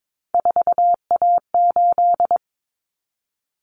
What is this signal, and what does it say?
Morse code, heard as a keyed tone, about 22 wpm, 700 Hz. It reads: 4A8
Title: Common Cut Numbers - 3-Digit 22wpm